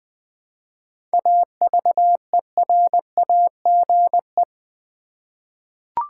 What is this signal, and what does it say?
Morse code, heard as a keyed tone, about 20 wpm, 700 Hz. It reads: AVERAGE
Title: 7 Letter Words - First 3 Letters Given - 20wpm